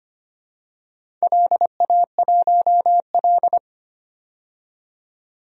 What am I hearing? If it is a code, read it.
Morse code, heard as a keyed tone, about 25 wpm, 700 Hz. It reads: LA1L
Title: DXCC-100 Call Signs - 4 Characters - 25wpm